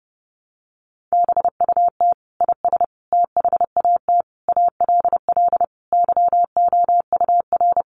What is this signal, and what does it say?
Morse code, heard as a keyed tone, about 30 wpm, 700 Hz. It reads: BUT IS THAT ALL YOUR
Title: Sentences from Top 1300 Words 30wpm